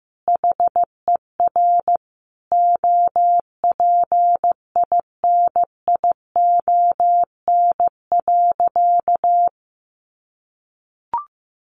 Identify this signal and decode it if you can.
Morse code, heard as a keyed tone, about 15 wpm, 700 Hz. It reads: HER OPINION.
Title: Sentences from Top 1400 Words 15wpm